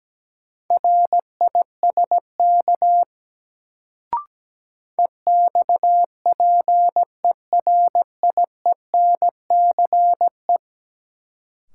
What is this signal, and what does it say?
Morse code, heard as a keyed tone, about 17 wpm, 700 Hz. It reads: RISK EXPERIENCE
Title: Sentences from Top 600 Words - New Word Drill - 17wpm